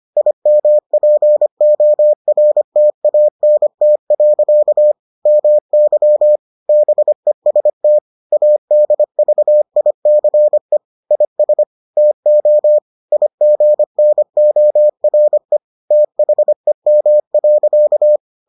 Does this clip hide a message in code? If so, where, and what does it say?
Yes: Morse code from 0.2 to 18.2 s, heard as a keyed tone, about 25 wpm, 600 Hz. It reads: IMPORTANT. MY BEST ADVICE IS TO IGNORE THEM.